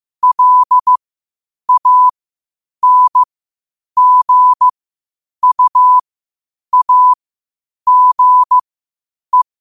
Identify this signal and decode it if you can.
Morse code, heard as a keyed tone, about 15 wpm, spaced as for 8 wpm, 1000 Hz. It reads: LANGUAGE
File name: qotdfiles / 8wpm